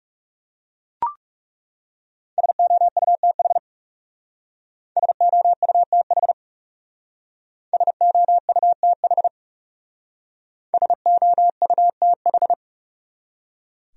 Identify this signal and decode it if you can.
Morse code, heard as a keyed tone, about 39 wpm, 700 Hz. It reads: SOUTH SOUTH SOUTH SOUTH